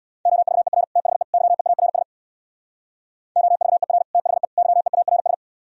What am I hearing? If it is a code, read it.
Morse code, heard as a keyed tone, about 64 wpm, 700 Hz. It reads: OVER THE YEARS OVER THE YEARS